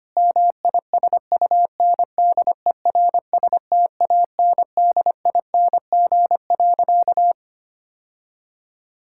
Morse code, heard as a keyed tone, about 25 wpm, 700 Hz. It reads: MISUNDERSTANDING.
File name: practice / 25wpm / Sentences with 4-Word Phrases - 25wpm